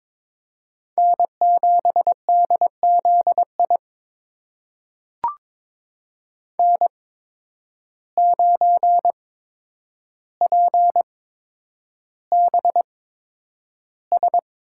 Morse code, heard as a keyed tone, about 22 wpm, spaced as for 9 wpm, 700 Hz. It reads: N7DZI N9PBS
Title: USA Call Signs - 5 Characters - 8x Character Spacing - 22wpm